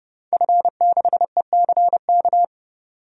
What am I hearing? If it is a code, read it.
Morse code, heard as a keyed tone, about 30 wpm, 700 Hz. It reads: F6ECK
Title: DXCC-100 Call Signs - 5 Characters - 30wpm